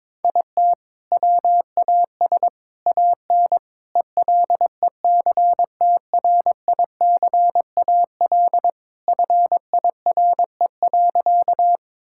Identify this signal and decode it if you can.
Morse code, heard as a keyed tone, about 22 wpm, 700 Hz. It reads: IT WAS AN ELECTRICAL FIRE.
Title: Sentences from Top 1000 Words - Review Part 6 - 22wpm